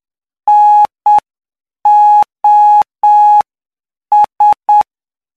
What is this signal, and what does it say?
Morse code, heard as a keyed tone, about 8 wpm, spaced as for 7 wpm, 840 Hz. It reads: NOS
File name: cw / py2lcd / 13-PY2LCD-Li��o 22